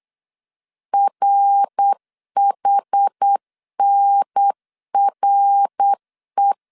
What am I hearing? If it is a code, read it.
Morse code, heard as a keyed tone, about 8 wpm, 800 Hz. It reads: RHNRE